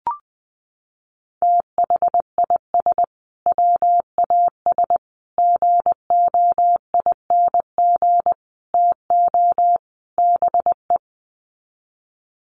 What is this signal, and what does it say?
Morse code, heard as a keyed tone, about 20 wpm, 700 Hz. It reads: THIS WAS GOING TO BE
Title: Sets of 5 Words Encore2 20wpm